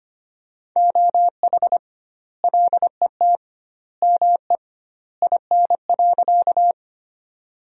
Morse code, heard as a keyed tone, about 25 wpm, 700 Hz. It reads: OH LET ME IN.
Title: 2x Word-Spacing - Sentences from Top 500 Words - 25wpm